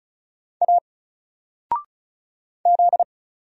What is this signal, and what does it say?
Morse code, heard as a keyed tone, about 35 wpm, 700 Hz. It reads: A Z
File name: Single-Letter-Number-35wpm